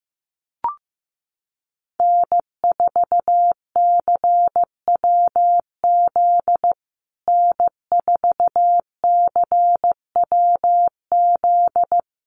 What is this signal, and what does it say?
Morse code, heard as a keyed tone, about 15 wpm, 700 Hz. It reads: N4CWZ N4CWZ
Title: USA Call Signs Repeated 3x 15wpm